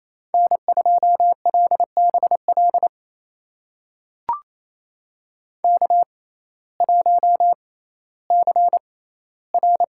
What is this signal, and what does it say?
Morse code, heard as a keyed tone, about 28 wpm, spaced as for 15 wpm, 700 Hz. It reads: N2LBL K1CR
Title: USA Call Signs - 5 Characters - 6x Character Spacing - 28wpm